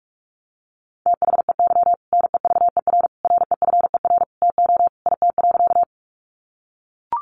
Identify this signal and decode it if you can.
Morse code, heard as a keyed tone, about 44 wpm, 700 Hz. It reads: THEY NEVER REFER TO IT.
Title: Sentences from Top 1000 Words 45wpm